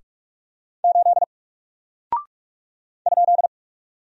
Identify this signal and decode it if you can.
Morse code, heard as a keyed tone, about 44 wpm, 700 Hz. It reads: 8 ?
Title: Single Character (Letter:Number:Punctuation:Prosign) Alt 45wpm